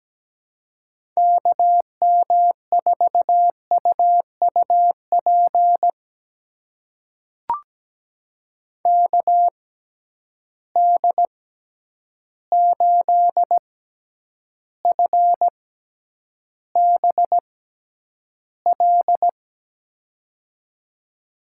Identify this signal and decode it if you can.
Morse code, heard as a keyed tone, about 17 wpm, spaced as for 9 wpm, 700 Hz. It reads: KM4UUP KD8FBL
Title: USA Call Signs - 6 Characters - 6x Character Spacing - 17wpm